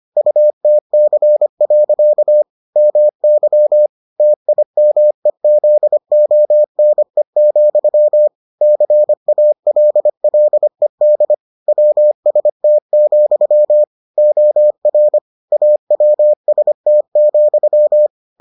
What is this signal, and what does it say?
Morse code, heard as a keyed tone, about 25 wpm, 600 Hz. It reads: UTC. MY TIMEZONE, CALLED WST, OR AWST,